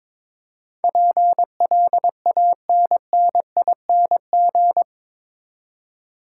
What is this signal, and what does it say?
Morse code, heard as a keyed tone, about 22 wpm, 700 Hz. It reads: PLANNING
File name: practice / 22wpm / Eight Letter Words - 22wpm